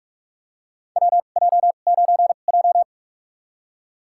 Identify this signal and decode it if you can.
Morse code, heard as a keyed tone, about 44 wpm, spaced as for 32 wpm, 700 Hz. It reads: WJ9J
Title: USA Call Signs - 4 Characters - 2x Character Spacing - 45wpm